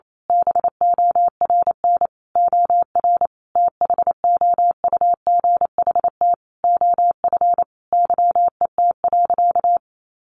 Morse code, heard as a keyed tone, about 28 wpm, 700 Hz. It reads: BORN OR THOUGHT OF YET.
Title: A Sound Of Thunder 28wpm